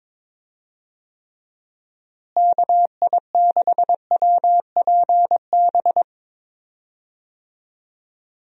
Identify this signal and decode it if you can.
Morse code, heard as a keyed tone, about 22 wpm, 700 Hz. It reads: KI6WPB
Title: USA Call Signs - 6 Characters - Repeated 3x - 22wpm